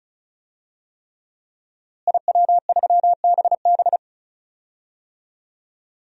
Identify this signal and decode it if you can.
Morse code, heard as a keyed tone, about 36 wpm, 700 Hz. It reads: IW3BB
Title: DXCC-100 Call Signs - 5 Characters - Repeated 3x - 35wpm